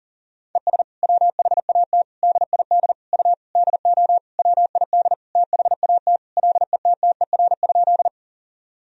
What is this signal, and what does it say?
Morse code, heard as a keyed tone, about 40 wpm, 700 Hz. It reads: ES WHAT DID U DO WID THAT LETTER?
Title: Sentences from Top 300 Words and Abbreviations 40wpm